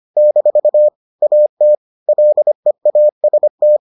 Morse code, heard as a keyed tone, about 25 wpm, 600 Hz. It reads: - AT LEAST